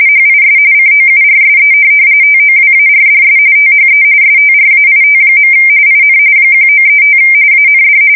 RTTY, about 45.45 baud. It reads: HIS COURSE TOWARDS THE STRAIT. IN 1935 AND
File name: fsk170 / USS-New-Jersey